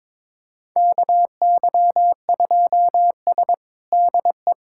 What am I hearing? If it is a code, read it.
Morse code, heard as a keyed tone, about 21 wpm, 700 Hz. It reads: KY2S DE
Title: USA Callsign DE Callsign - 4 Characters - 22wpm